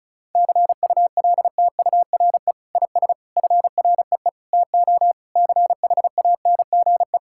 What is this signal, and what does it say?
Morse code, heard as a keyed tone, about 36 wpm, 700 Hz. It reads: CULTURE IS FREE TO CHANGE
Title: Sentences from Top 1000 Words - Review Part 6 - 35wpm